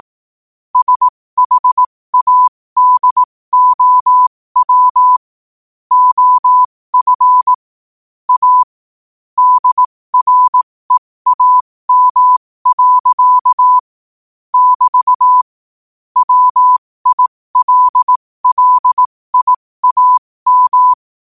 Morse code, heard as a keyed tone, about 18 wpm, spaced as for 15 wpm, 1000 Hz. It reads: SHADOW OF A DREAM. <BT> WILLIAM
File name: qotdfiles / 15wpm